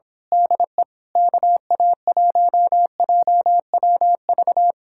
Morse code, heard as a keyed tone, about 26 wpm, 700 Hz. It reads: DE KA1JWV
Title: Speed-Racing Common QSO Phrases Encore - 22wpm